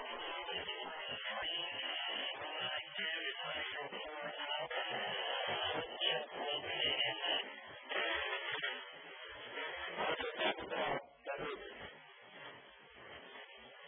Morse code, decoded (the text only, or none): none